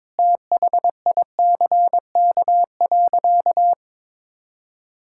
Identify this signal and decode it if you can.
Morse code, heard as a keyed tone, about 22 wpm, 700 Hz. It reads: THICK.
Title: Sentences from Top 1600 Words 22wpm